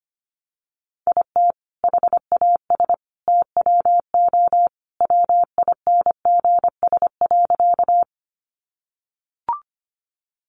Morse code, heard as a keyed tone, about 25 wpm, 700 Hz. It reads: IT HAS TWO WINGS.
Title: Additional Easy Phrases 25wpm